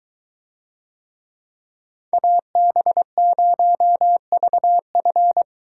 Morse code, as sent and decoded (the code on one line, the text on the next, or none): .- -... ----- ...- ..-.
AB0VF